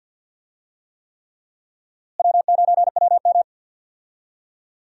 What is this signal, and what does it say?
Morse code, heard as a keyed tone, about 48 wpm, 700 Hz. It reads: W9WM
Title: USA Call Signs - 4 Characters - Repeated 3x - 50wpm